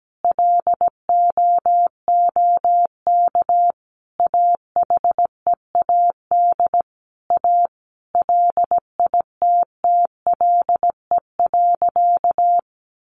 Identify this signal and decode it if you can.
Morse code, heard as a keyed tone, about 17 wpm, 700 Hz. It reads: LOOK AHEAD A LITTLE.
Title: Sentences from Top 1100 Words 17wpm